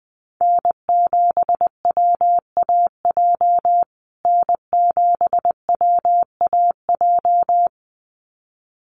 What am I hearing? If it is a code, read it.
Morse code, heard as a keyed tone, about 20 wpm, 700 Hz. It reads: N7WAJ N7WAJ
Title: USA Call Signs Repeated 2x 20wpm